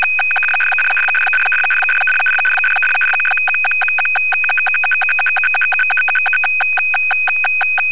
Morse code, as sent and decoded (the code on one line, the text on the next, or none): none